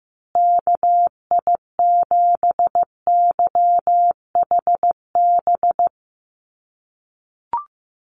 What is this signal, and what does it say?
Morse code, heard as a keyed tone, about 14 wpm, 700 Hz. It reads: KI7YHB
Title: USA Call Signs Repeated 3x 15wpm